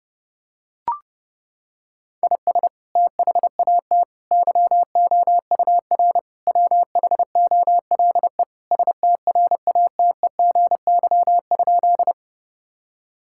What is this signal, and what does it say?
Morse code, heard as a keyed tone, about 30 wpm, 700 Hz. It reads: IS THAT YOUR WHOLE STRATEGY?
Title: Sentences from Top 1000 Words - Review Part 5 - 30wpm